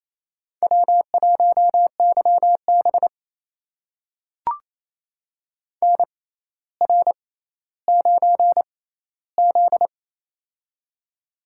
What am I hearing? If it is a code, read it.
Morse code, heard as a keyed tone, about 28 wpm, spaced as for 14 wpm, 700 Hz. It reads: W1YB NR9Z
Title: USA Call Signs - 4 Characters - 6x Character Spacing - 28wpm